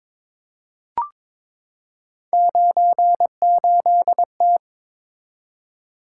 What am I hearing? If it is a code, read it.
Morse code, heard as a keyed tone, about 22 wpm, 700 Hz. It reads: 98T
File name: practice / 22wpm / Three Letter-Number Combinations - 22wpm